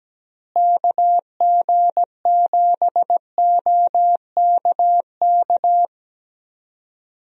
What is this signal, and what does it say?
Morse code, heard as a keyed tone, about 17 wpm, 700 Hz. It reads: KG7OKK
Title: USA Call Signs - 6 Characters - Repeated 2x - 17wpm